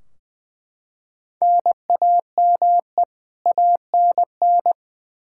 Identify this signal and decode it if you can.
Morse code, heard as a keyed tone, about 20 wpm, 700 Hz. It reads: NAME ANN
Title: K1USN SST QSO Practice 20wpm